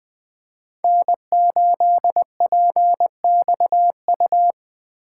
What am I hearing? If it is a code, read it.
Morse code, heard as a keyed tone, about 21 wpm, 700 Hz. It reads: N8PXU
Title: Speed Racing - USA 1x3 Call Signs 17wpm